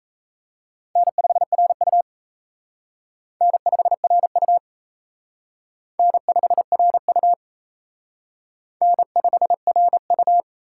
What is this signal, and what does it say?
Morse code, heard as a keyed tone, about 34 wpm, 700 Hz. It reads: N5RU N5RU N5RU N5RU
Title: Speed Racing - USA 1x2 Call Signs 28wpm